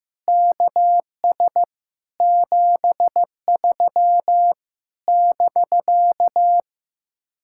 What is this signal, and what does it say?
Morse code, heard as a keyed tone, about 15 wpm, 700 Hz. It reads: KS 73 <BK>
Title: Speed-Racing Common QSO Phrases Encore - 15wpm